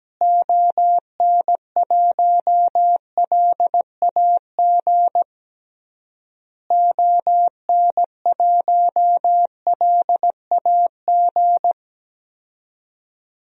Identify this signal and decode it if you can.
Morse code, heard as a keyed tone, about 17 wpm, 700 Hz. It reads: ON1LAG ON1LAG
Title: DXCC-100 Call Signs - 6 Characters - Repeated 3x - 17wpm